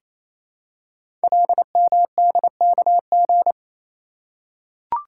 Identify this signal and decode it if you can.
Morse code, heard as a keyed tone, about 27 wpm, 700 Hz. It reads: LMDKG
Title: Code Groups - 5 Letters - 28wpm